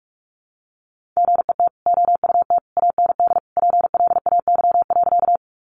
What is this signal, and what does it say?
Morse code, heard as a keyed tone, about 44 wpm, 700 Hz. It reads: GET OUT AND PLAY.
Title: Sentences from Top 200 Words 45wpm